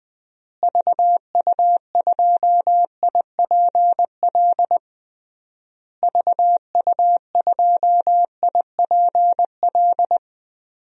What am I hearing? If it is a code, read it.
Morse code, heard as a keyed tone, about 21 wpm, 700 Hz. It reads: VU2IPL VU2IPL
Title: DXCC-100 Call Signs - 6 Characters - Repeated 3x - 20wpm